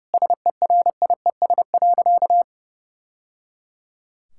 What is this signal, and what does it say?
Morse code, heard as a keyed tone, about 30 wpm, 700 Hz. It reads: SERIES.
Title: Sentences from Top 1000 Words - Review Part 5 - 30wpm